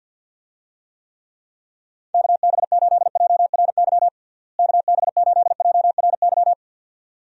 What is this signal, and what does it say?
Morse code, heard as a keyed tone, about 52 wpm, 700 Hz. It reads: KB8JRY KB8JRY
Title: USA Call Signs - 6 Characters - Repeated 2x - 50wpm